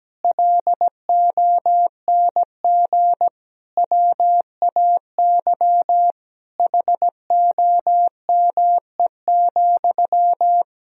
Morse code, heard as a keyed tone, about 17 wpm, 700 Hz. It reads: LONG WAY HOME,
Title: Sentences from Top 1000 Words - Review Part 2 - 17wpm